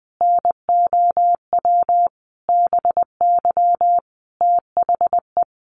Morse code, heard as a keyed tone, about 20 wpm, 700 Hz. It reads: NOW BY THE
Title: Sentences from Top 300 Words 20wpm